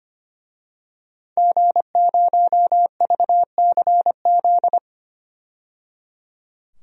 Morse code, heard as a keyed tone, about 25 wpm, 700 Hz. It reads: G0VCZ
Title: DXCC-100 Call Signs - 5 Characters - 25wpm